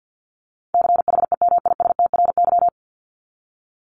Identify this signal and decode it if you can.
Morse code, heard as a keyed tone, about 50 wpm, 700 Hz. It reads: CHEMISTRY
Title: Top 751-1000 Words 50wpm